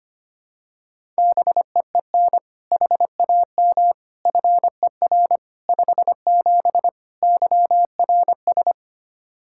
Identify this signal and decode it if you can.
Morse code, heard as a keyed tone, about 25 wpm, 700 Hz. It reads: BEEN HAM FER 57 YRS
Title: K1USN SST QSO Practice 25wpm